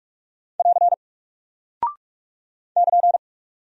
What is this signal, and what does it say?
Morse code, heard as a keyed tone, about 44 wpm, 700 Hz. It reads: <AR> <KN>
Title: Single Character (Letter:Number:Punctuation:Prosign) Alt 45wpm